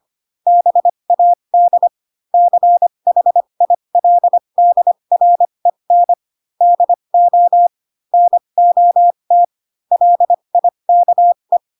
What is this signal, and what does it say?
Morse code, heard as a keyed tone, about 25 wpm, spaced as for 22 wpm, 700 Hz. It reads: BAD CHILDREN DO NOT LIKE
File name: Pinocchio / Pinocchio ch4 18 25 ik1hge